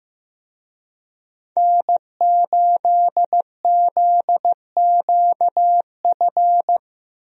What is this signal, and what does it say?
Morse code, heard as a keyed tone, about 15 wpm, 700 Hz. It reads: N8ZQF
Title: USA Call Signs - 5 Characters - Repeated 3x - 15wpm